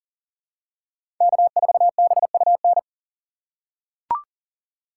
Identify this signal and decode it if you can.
Morse code, heard as a keyed tone, about 40 wpm, 700 Hz. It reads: K4BUN
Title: USA Call Signs - 5 Characters - 6x Character Spacing - 40wpm